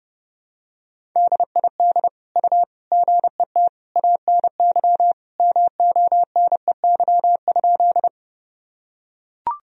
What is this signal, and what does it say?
Morse code, heard as a keyed tone, about 32 wpm, 700 Hz. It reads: DID U GET ANY MONEY?
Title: Sentences from Top 400 Words and Abbreviations 30wpm